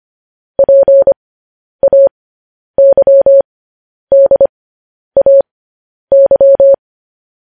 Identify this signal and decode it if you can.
Morse code, heard as a keyed tone, about 25 wpm, spaced as for 10 wpm, 550 Hz. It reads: PAYDAY
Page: news-headlines-in-morse-code-at-10-wpm